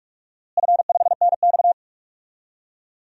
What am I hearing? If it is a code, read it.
Morse code, heard as a keyed tone, about 42 wpm, 700 Hz. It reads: F5NX
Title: DXCC-100 Call Signs - 4 Characters - Repeated 2x - 45wpm